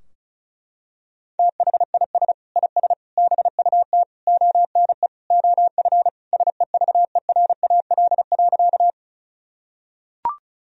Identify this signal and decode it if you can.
Morse code, heard as a keyed tone, about 34 wpm, 700 Hz. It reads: THIS IS BUT ONE OF SEVERAL.
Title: Sentences from Top 1000 Words - Review Part 2 - 35wpm